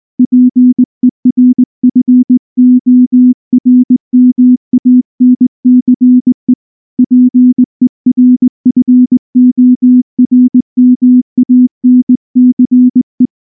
Morse code, heard as a keyed tone, about 19 wpm, 250 Hz. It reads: PERFORMANCE PERFORMANCE
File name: morse / morse-training-20WPM-1